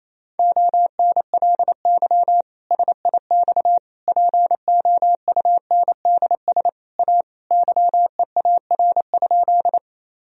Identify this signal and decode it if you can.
Morse code, heard as a keyed tone, about 28 wpm, 700 Hz. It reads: ONLY SIX POUNDS A YEAR?
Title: Sentences from Top 1000 Words - Review Part 2 - 28wpm